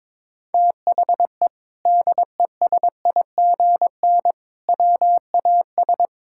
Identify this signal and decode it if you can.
Morse code, heard as a keyed tone, about 22 wpm, 700 Hz. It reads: THE DESIGN WAS
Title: Sentences from Top 1000 Words - Review Part 5 - 22wpm